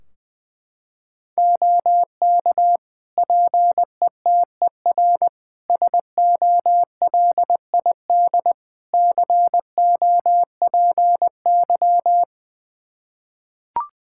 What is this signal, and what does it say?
Morse code, heard as a keyed tone, about 20 wpm, 700 Hz. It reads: OK PETER SOLID COPY
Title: K1USN SST QSO Practice 20wpm